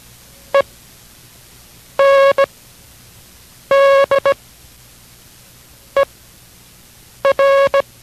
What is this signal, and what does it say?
Morse code, heard as a keyed tone, about 17 wpm, spaced as for 6 wpm, 550 Hz. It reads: ENDER